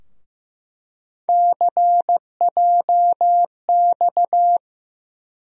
Morse code, heard as a keyed tone, about 15 wpm, 700 Hz. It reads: CJX
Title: USA Call Sign Postfixes - 3 Letters 15wpm